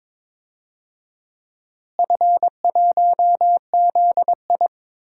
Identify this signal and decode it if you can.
Morse code, heard as a keyed tone, about 22 wpm, 700 Hz. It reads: F1ZI
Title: DXCC-100 Call Signs - 4 Characters - Repeated 3x - 22wpm